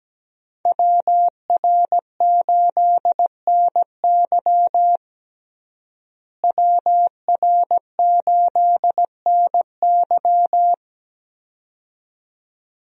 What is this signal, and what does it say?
Morse code, heard as a keyed tone, about 17 wpm, 700 Hz. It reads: WR8NY WR8NY
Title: USA Call Signs - 5 Characters - Repeated 2x - 17wpm